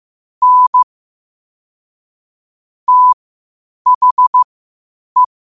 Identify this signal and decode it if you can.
Morse code, heard as a keyed tone, about 15 wpm, spaced as for 8 wpm, 1000 Hz. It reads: N THE
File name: qotdfiles / 8wpm